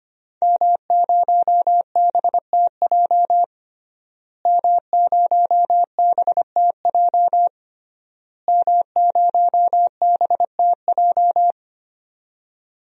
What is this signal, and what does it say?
Morse code, heard as a keyed tone, about 25 wpm, 700 Hz. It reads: M0BTJ M0BTJ M0BTJ
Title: DXCC-100 Call Signs - 5 Characters - Repeated 3x - 25wpm